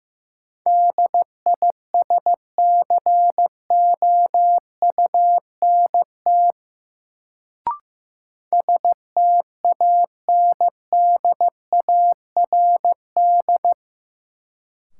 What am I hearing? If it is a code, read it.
Morse code, heard as a keyed tone, about 15 wpm, 700 Hz. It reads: DISCOUNT STANDARD
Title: Sentences from Top 1200 Words - New Word Drill Alt - 15wpm